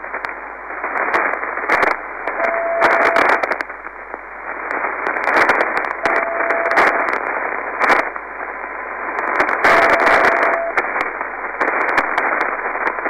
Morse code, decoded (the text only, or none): none